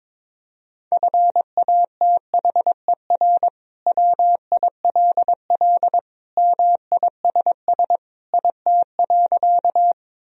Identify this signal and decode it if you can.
Morse code, heard as a keyed tone, about 21 wpm, 700 Hz. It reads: FATHER WILL MISS IT.
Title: Sentences from Top 1000 Words - Review Part 3 - 22wpm